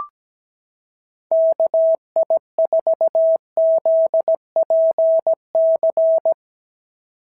Morse code, heard as a keyed tone, about 17 wpm, 650 Hz. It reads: KI4ZPC
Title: SOTA QSO Practice 17wpm